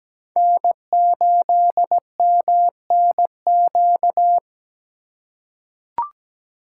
Morse code, heard as a keyed tone, about 17 wpm, 700 Hz. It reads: N8MNQ
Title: USA Call Signs - 5 Characters - 2x Character Spacing - 17wpm